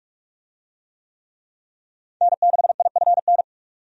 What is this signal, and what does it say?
Morse code, heard as a keyed tone, about 46 wpm, 700 Hz. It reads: N6IFN
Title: USA Call Signs - 5 Characters - 45wpm